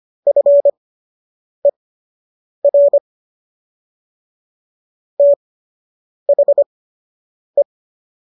Morse code, heard as a keyed tone, about 25 wpm, spaced as for 8 wpm, 570 Hz. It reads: FER THE